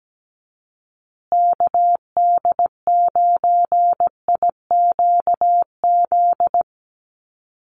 Morse code, heard as a keyed tone, about 17 wpm, 700 Hz. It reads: KD9IQZ
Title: USA Call Signs Repeated 2x 17wpm